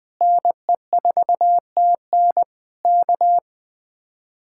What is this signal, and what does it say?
Morse code, heard as a keyed tone, about 20 wpm, 700 Hz. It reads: NE4TN K